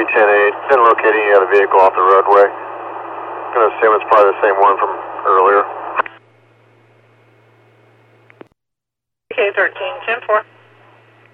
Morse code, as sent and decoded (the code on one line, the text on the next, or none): none